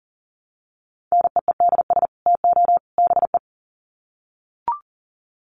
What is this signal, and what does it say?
Morse code, heard as a keyed tone, about 40 wpm, 700 Hz. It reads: NEEDS TO BE
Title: Sets of 3 Words 40wpm